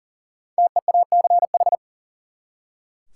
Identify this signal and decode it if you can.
Morse code, heard as a keyed tone, about 40 wpm, 700 Hz. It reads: TEACH
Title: Five Letter Words - 40wpm